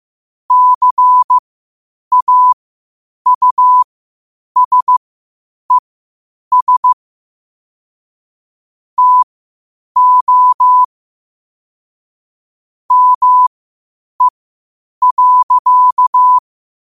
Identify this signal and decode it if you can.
Morse code, heard as a keyed tone, about 15 wpm, spaced as for 8 wpm, 1000 Hz. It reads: CAUSES TO ME.
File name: qotdfiles / 8wpm